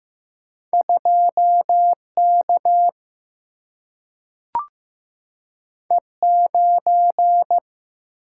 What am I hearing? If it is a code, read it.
Morse code, heard as a keyed tone, about 15 wpm, 700 Hz. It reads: 2K E9
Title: Single-Letter Single-Number 15wpm